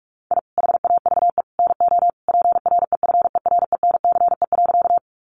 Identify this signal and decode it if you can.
Morse code, heard as a keyed tone, about 44 wpm, 700 Hz. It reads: I HAVE NO PREFERENCE.
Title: Sentences from Top 1500 Words 45wpm